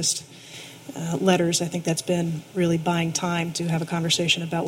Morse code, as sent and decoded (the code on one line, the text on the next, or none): none